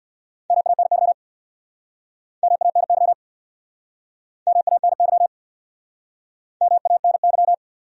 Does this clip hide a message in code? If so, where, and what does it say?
Yes: Morse code from 0.5 to 7.5 s, heard as a keyed tone, about 69 wpm, 700 Hz. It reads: MANY MANY MANY MANY